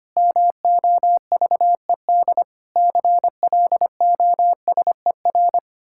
Morse code, heard as a keyed tone, about 25 wpm, 700 Hz. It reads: MOVED CLOSER
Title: Pangram Challenge v2 - 25wpm